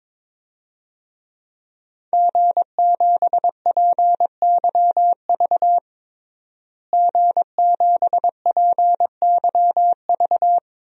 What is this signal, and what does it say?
Morse code, heard as a keyed tone, about 22 wpm, 700 Hz. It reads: G7PYV G7PYV
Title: DXCC-100 Call Signs - 5 Characters - Repeated 2x - 22wpm